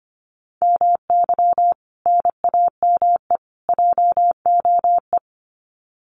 Morse code, heard as a keyed tone, about 25 wpm, 700 Hz. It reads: MY NAME JOE
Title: Common QSO Phrases 25wpm